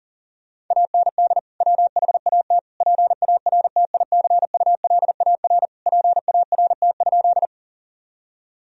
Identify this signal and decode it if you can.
Morse code, heard as a keyed tone, about 40 wpm, 700 Hz. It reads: AND WHAT PARTICULAR PART?
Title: Sentences from Top 1000 Words - Review Part 4 - 40wpm